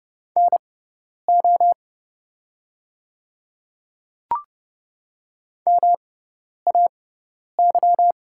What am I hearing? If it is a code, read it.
Morse code, heard as a keyed tone, about 30 wpm, spaced as for 10 wpm, 700 Hz. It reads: NO MAY
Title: 6x Character Spacing - Top 100 Words - 30wpm